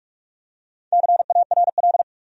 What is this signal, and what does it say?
Morse code, heard as a keyed tone, about 46 wpm, 700 Hz. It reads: CARL
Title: USA Common Names - Max 4 Letters - 45wpm